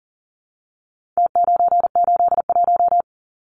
Morse code, heard as a keyed tone, about 40 wpm, 700 Hz. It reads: T981
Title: Numbers - Contest Style - 4-Digit 40wpm